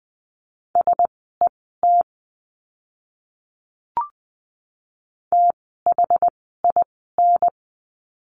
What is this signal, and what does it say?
Morse code, heard as a keyed tone, about 20 wpm, spaced as for 14 wpm, 700 Hz. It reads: SET THIN
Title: 068 Lesson 12f - Letter d - Words 20wpm